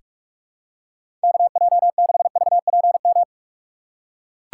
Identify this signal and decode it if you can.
Morse code, heard as a keyed tone, about 42 wpm, 700 Hz. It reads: KJ6VPM
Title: USA Call Signs - 2x Character Spacing - 45wpm